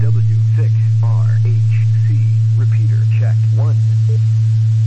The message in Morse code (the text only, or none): none